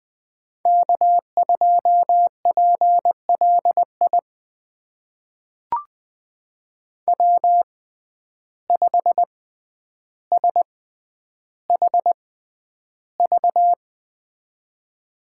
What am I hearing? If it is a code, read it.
Morse code, heard as a keyed tone, about 20 wpm, spaced as for 10 wpm, 700 Hz. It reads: K2PLI W5SHV
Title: USA Call Signs - 5 Characters - 6x Character Spacing - 20wpm